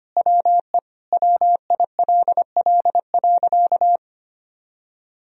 Morse code, heard as a keyed tone, about 25 wpm, 700 Hz. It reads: WE WILL.
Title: Sentences from Top 1000 Words - Review Part 1 - 25wpm